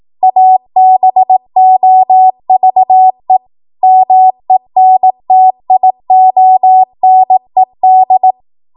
Morse code, heard as a keyed tone, about 18 wpm, 750 Hz. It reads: ABOVE MENTIONED
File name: Morse / 18WPM